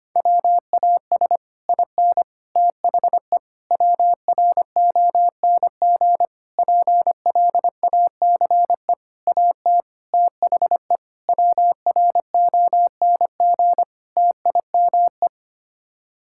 Morse code, heard as a keyed tone, about 25 wpm, 700 Hz. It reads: WAS IN THE WRONG PLACE AT THE WRONG TIME
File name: Sets of 9 Words - 25wpm